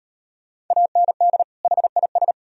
Morse code, heard as a keyed tone, about 38 wpm, 700 Hz. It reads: AND HIS